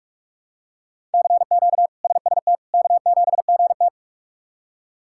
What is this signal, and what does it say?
Morse code, heard as a keyed tone, about 47 wpm, 700 Hz. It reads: CQ SST K7GT
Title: K1USN SST QSO Practice - 45wpm